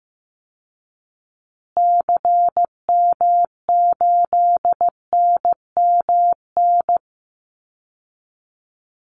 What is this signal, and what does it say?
Morse code, heard as a keyed tone, about 15 wpm, 700 Hz. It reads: CM8NMN
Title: Call Signs Repeated 3x - 15wpm